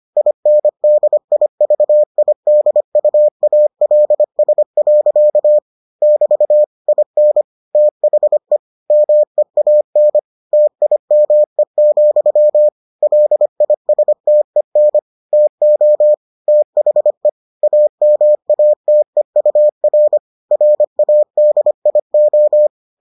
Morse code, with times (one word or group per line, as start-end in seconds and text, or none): 0.2-5.6 INDIVIDUALS.
6.0-6.6 <BT>
6.9-7.4 IN
7.8-8.6 THE
8.9-10.2 MEAN
10.5-12.7 TIME,
13.0-15.0 LISTEN
15.3-16.1 TO
16.5-17.3 THE
17.6-20.2 AMATEUR
20.5-22.7 RADIO